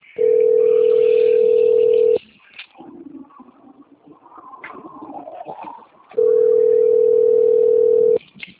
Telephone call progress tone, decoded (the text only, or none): ringback tone